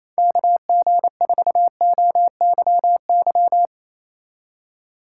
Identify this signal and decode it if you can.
Morse code, heard as a keyed tone, about 37 wpm, 700 Hz. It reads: KG4OYY